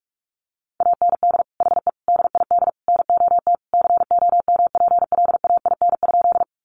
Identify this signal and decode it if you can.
Morse code, heard as a keyed tone, about 44 wpm, 700 Hz. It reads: AND HE DID NOT COMPLAIN?
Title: Sentences from Top 1300 Words 45wpm